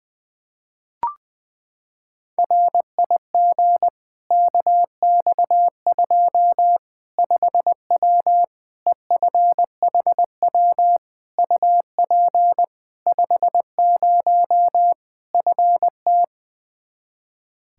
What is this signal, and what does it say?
Morse code, heard as a keyed tone, about 20 wpm, 700 Hz. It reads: RIG KX2 5W EFHW UP 50 FT
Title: Practice QSO - Exchange 2 - Info - 20wpm